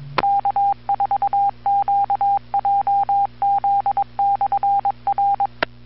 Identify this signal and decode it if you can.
Morse code, heard as a keyed tone, about 22 wpm, 800 Hz. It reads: K4QJZ/R
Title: Bluemont, VA Skywarn Repeater - WA4TSC